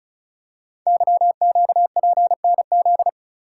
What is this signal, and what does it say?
Morse code, heard as a keyed tone, about 36 wpm, 690 Hz. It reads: YQPNZ
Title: Code Groups - 5 Letters - 35wpm